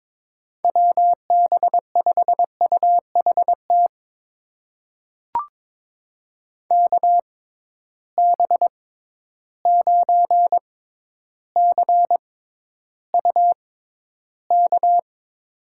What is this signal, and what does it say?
Morse code, heard as a keyed tone, about 22 wpm, spaced as for 11 wpm, 700 Hz. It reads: WB5UHT KB9CUK
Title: USA Call Signs - 6 Characters - 6x Character Spacing - 22wpm